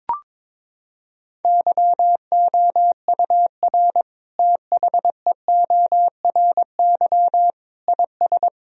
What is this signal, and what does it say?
Morse code, heard as a keyed tone, about 22 wpm, 700 Hz. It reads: YOUR THEORY IS
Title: Sentences from Top 1000 Words - Review Part 3 - 22wpm